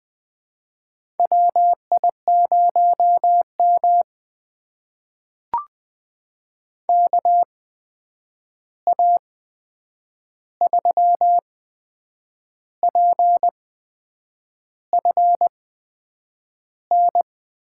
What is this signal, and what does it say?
Morse code, heard as a keyed tone, about 20 wpm, spaced as for 8 wpm, 700 Hz. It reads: WI0M KA3PFN
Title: USA Call Signs - 8x Character Spacing - 20wpm